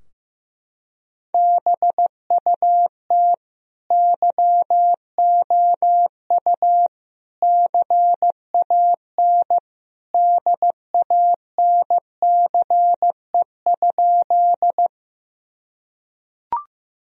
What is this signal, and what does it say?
Morse code, heard as a keyed tone, about 15 wpm, 700 Hz. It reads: BUT YOU CAN DANCE?
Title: Sentences from Top 1000 Words - Review Part 4 - 15wpm